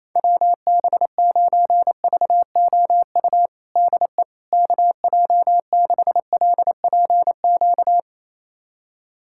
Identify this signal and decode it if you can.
Morse code, heard as a keyed tone, about 27 wpm, 700 Hz. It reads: WB9VOU DE KJ6LPQ
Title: USA Callsign DE Callsign - 6 Characters - 28wpm